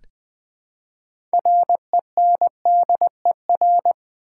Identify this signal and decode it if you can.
Morse code, heard as a keyed tone, about 20 wpm, 700 Hz. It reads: RENDER
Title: Build A Word - 3 Letters At A Time - 6-Letter Words - 20wpm